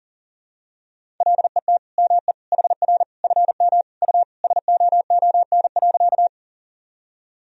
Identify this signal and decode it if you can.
Morse code, heard as a keyed tone, about 40 wpm, 700 Hz. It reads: LET ME HR FM U SOON.
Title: Sentences from Top 400 Words and Abbreviations 40wpm